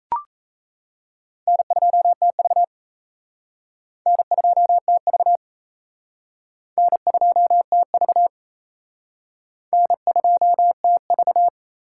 Morse code, heard as a keyed tone, about 34 wpm, 700 Hz. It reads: N2TV N2TV N2TV N2TV
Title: Speed Racing - USA 1x2 Call Signs 28wpm